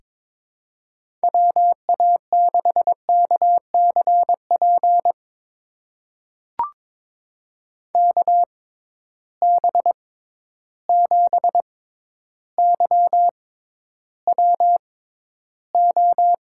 Morse code, heard as a keyed tone, about 21 wpm, spaced as for 11 wpm, 700 Hz. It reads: WA6KCP KB7YWO
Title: USA Call Signs - 6 Characters - 6x Character Spacing - 22wpm